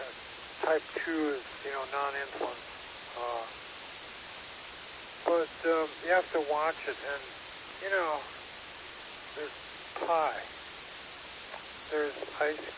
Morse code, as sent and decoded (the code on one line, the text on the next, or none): none